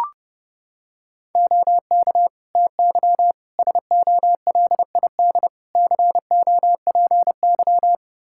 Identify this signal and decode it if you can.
Morse code, heard as a keyed tone, about 32 wpm, 700 Hz. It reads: OK TY SOLID COPY
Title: K1USN SST QSO Practice 30wpm